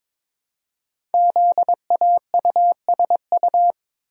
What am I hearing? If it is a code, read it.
Morse code, heard as a keyed tone, about 22 wpm, 700 Hz. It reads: ZAUSU
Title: Code Groups - 5 Letters - 22wpm